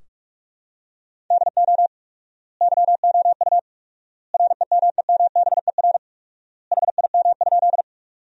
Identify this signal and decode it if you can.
Morse code, heard as a keyed tone, about 46 wpm, 700 Hz. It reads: DO YOU REMEMBER HIM?
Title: 4x Word-Spacing - Sentences from Top 400 Words - 45wpm